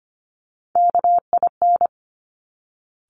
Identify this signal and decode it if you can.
Morse code, heard as a keyed tone, about 25 wpm, 700 Hz. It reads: KIN
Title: Top Three Letter Combinations in Words 25wpm